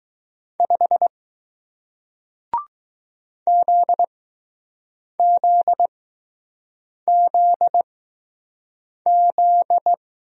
Morse code, heard as a keyed tone, about 20 wpm, spaced as for 6 wpm, 700 Hz. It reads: 5 ZZZZ